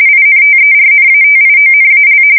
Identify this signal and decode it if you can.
RTTY, about 45.45 baud. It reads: INTERNATIONAL
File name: fsk170 / USS-New-Jersey